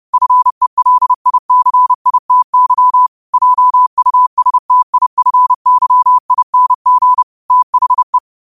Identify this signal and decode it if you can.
Morse code, heard as a keyed tone, about 30 wpm, 1000 Hz. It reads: FELICITY JUSTIFYING THE
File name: qotdfiles / 30wpm